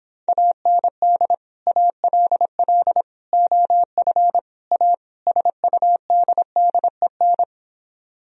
Morse code, heard as a keyed tone, about 26 wpm, 700 Hz. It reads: AND ALL OF A SUDDEN